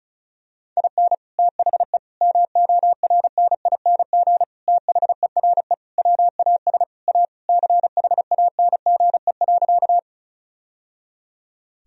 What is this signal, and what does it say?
Morse code, heard as a keyed tone, about 36 wpm, 700 Hz. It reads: IN THE MORNING THERE WAS A CHANGE.
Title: Sentences from Top 1000 Words - Review Part 1 - 35wpm